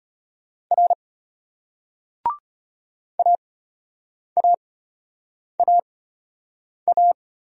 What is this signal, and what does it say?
Morse code, heard as a keyed tone, about 34 wpm, spaced as for 8 wpm, 700 Hz. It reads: R AAAA